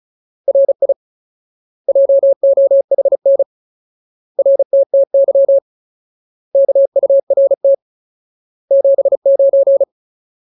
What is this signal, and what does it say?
Morse code, heard as a keyed tone, about 34 wpm, 550 Hz. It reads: RI JOHN RTTY KURT 79
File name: Ultimate Word List / 35WPM Ultimate